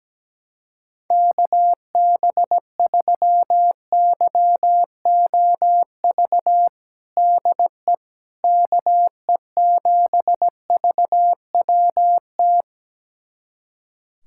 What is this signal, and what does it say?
Morse code, heard as a keyed tone, about 17 wpm, 700 Hz. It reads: KB3YOV DE KE7VWT
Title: USA Callsign DE Callsign - 6 Characters - 17wpm